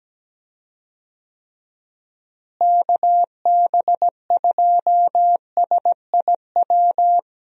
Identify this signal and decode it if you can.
Morse code, heard as a keyed tone, about 17 wpm, 700 Hz. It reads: KB2SIW
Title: USA Call Signs - 6 Characters - Repeated 3x - 17wpm